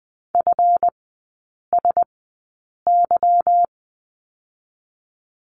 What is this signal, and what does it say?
Morse code, heard as a keyed tone, about 20 wpm, 700 Hz. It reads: F S Y
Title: 145 Lesson 24e - Letter y - Three Characters 20wpm